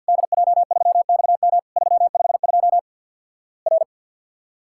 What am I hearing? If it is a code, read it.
Morse code, heard as a keyed tone, about 50 wpm, 700 Hz. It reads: DJ3XM 352 R
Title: Worked All Europe DX Contest - QTC Exchange with Repeats - 50wpm